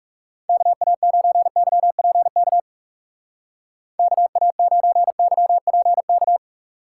Morse code, heard as a keyed tone, about 37 wpm, 700 Hz. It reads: KA9YPK KA9YPK
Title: Speed Racing - USA 2x3 Call Signs 30wpm